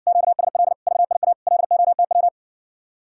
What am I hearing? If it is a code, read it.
Morse code, heard as a keyed tone, about 60 wpm, 700 Hz. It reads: QSL VIA LOTW